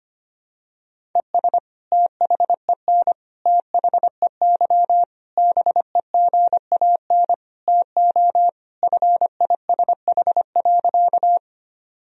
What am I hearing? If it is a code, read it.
Morse code, heard as a keyed tone, about 25 wpm, 700 Hz. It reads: ES THEN THEY BEGAN TO FISH.
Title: Sentences from Top 300 Words and Abbreviations 25wpm